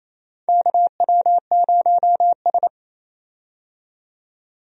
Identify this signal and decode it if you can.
Morse code, heard as a keyed tone, about 27 wpm, 700 Hz. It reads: KW0S
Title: USA Call Signs - 4 Characters - 28wpm